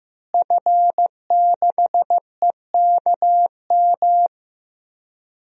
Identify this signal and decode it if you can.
Morse code, heard as a keyed tone, about 16 wpm, 700 Hz. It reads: F6EKM
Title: DXCC-100 Call Signs - 5 Characters - Repeated 2x - 15wpm